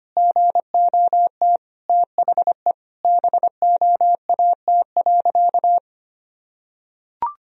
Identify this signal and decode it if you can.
Morse code, heard as a keyed tone, about 25 wpm, 700 Hz. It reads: GOT THE BOAT.
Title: Sentences from Top 1000 Words - Review Part 3 - 25wpm